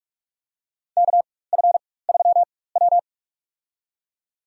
Morse code, heard as a keyed tone, about 46 wpm, spaced as for 21 wpm, 700 Hz. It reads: KF3W
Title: USA Call Signs - 4 Characters - 4x Character Spacing - 45wpm